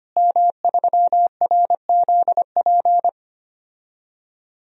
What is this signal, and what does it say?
Morse code, heard as a keyed tone, about 25 wpm, 700 Hz. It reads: M3RZP
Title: DXCC-100 Call Signs - 5 Characters - Repeated 2x - 25wpm